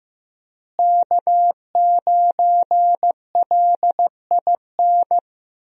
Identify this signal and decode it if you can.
Morse code, heard as a keyed tone, about 15 wpm, 700 Hz. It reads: K9LIN
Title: USA Call Signs - 5 Characters - Repeated 3x - 15wpm